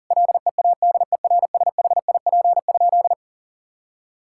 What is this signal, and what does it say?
Morse code, heard as a keyed tone, about 40 wpm, 700 Hz. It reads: LEADERSHIP?
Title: Sentences from Top 1600 Words 40wpm